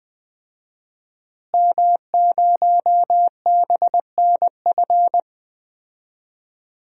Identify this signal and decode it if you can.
Morse code, heard as a keyed tone, about 20 wpm, 700 Hz. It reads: M0BNF
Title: DXCC-100 Call Signs - 5 Characters - Repeated 2x - 20wpm